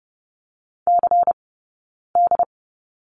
Morse code, heard as a keyed tone, about 30 wpm, spaced as for 9 wpm, 700 Hz. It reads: CD
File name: Lesson 20d - Number 3 - Two Characters 30wpm